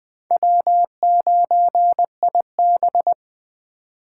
Morse code, heard as a keyed tone, about 21 wpm, 700 Hz. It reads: W9IB